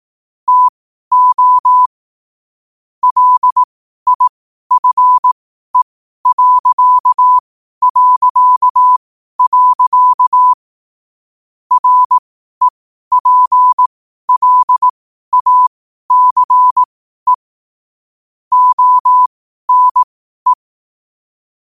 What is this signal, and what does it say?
Morse code, heard as a keyed tone, about 18 wpm, spaced as for 12 wpm, 1000 Hz. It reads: TO LIFE... REPLACE ONE
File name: qotdfiles / 12wpm